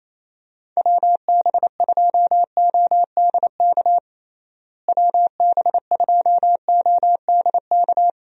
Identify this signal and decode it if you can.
Morse code, heard as a keyed tone, about 28 wpm, 700 Hz. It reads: WB2ODK WB2ODK
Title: USA Call Signs - 6 Characters - Repeated 2x - 28wpm